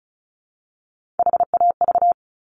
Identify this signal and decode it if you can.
Morse code, heard as a keyed tone, about 36 wpm, 700 Hz. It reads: HAV
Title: Top Three Letter Combinations in Words 35wpm